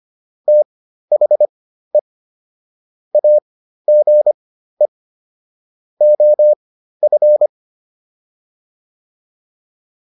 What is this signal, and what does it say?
Morse code, heard as a keyed tone, about 25 wpm, spaced as for 13 wpm, 600 Hz. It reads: THE AGE OF